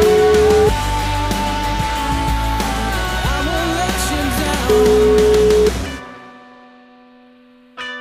Ringback tone (Europe, 425 Hz), with noise.